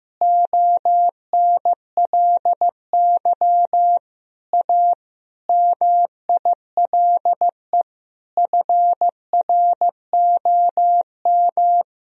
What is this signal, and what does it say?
Morse code, heard as a keyed tone, about 16 wpm, 700 Hz. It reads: ONLY A MILE FROM
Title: Sentences from Top 1000 Words - Review Part 1 - 15wpm